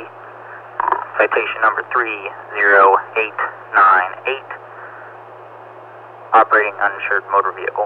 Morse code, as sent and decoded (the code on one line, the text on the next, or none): none